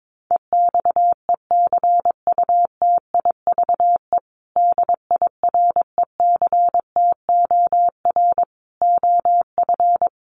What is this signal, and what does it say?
Morse code, heard as a keyed tone, about 22 wpm, 700 Hz. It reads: EXECUTIVE DIRECTOR OF
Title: Sets of 5 Words Encore2 22wpm